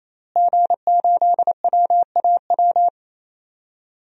Morse code, heard as a keyed tone, about 28 wpm, 700 Hz. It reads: G8WAW
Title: DXCC-100 Call Signs - 5 Characters - Repeated 3x - 28wpm